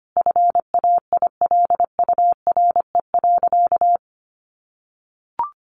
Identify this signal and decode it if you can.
Morse code, heard as a keyed tone, about 25 wpm, 700 Hz. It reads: FAILURE.
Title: Sentences from Top 900 Words 25wpm